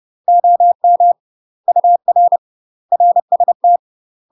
Morse code, heard as a keyed tone, about 30 wpm, 700 Hz. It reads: OM UR RST
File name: UK MASTER QSOs TRAINING / 30WPM / UK MASTER QSOs 0028